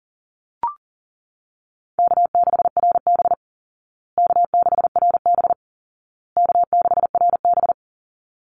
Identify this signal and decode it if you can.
Morse code, heard as a keyed tone, about 40 wpm, 700 Hz. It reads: K6RB K6RB K6RB
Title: Call Signs Repeated 2x - 40wpm